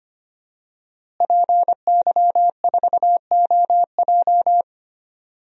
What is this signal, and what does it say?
Morse code, heard as a keyed tone, about 25 wpm, 700 Hz. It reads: PY4OJ
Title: DXCC-100 Call Signs - 5 Characters - Repeated 3x - 25wpm